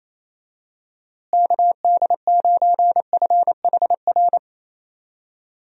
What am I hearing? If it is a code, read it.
Morse code, heard as a keyed tone, about 28 wpm, 700 Hz. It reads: KD9FHR
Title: USA Call Signs - 6 Characters - Repeated 2x - 28wpm